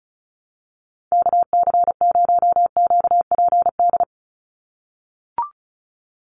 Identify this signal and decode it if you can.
Morse code, heard as a keyed tone, about 34 wpm, 700 Hz. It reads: KC0QPD
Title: USA Call Signs 35wpm